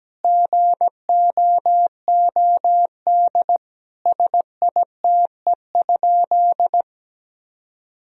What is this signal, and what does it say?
Morse code, heard as a keyed tone, about 17 wpm, 700 Hz. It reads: GOOD SITE?
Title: Sentences from Top 1000 Words - Review Part 4 - 17wpm